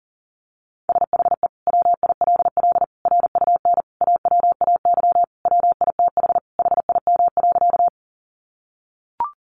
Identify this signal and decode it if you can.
Morse code, heard as a keyed tone, about 40 wpm, 700 Hz. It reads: SHE WILL RUN AWAY WITH HIM.
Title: Sentences from Top 600 Words 40wpm